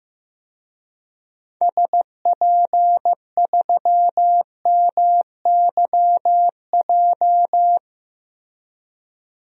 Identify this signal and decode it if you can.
Morse code, heard as a keyed tone, about 15 wpm, 700 Hz. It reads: SP3MYJ
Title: DXCC-100 Call Signs - 6 Characters - Repeated 3x - 15wpm